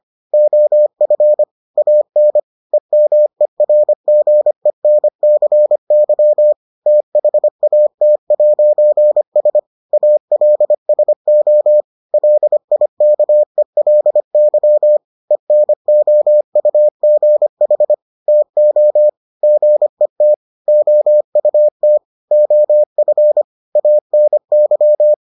Morse code, heard as a keyed tone, about 25 wpm, 600 Hz. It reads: OF AN EMERGENCY THAT'S ALSO LIKELY ENOUGH TO GET OUT OF ANY